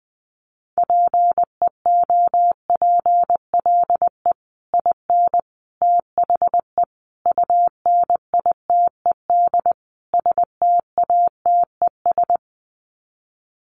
Morse code, heard as a keyed tone, about 20 wpm, 700 Hz. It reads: PEOPLE IN THE UNITED STATES
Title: Sets of 5 Words Encore 20wpm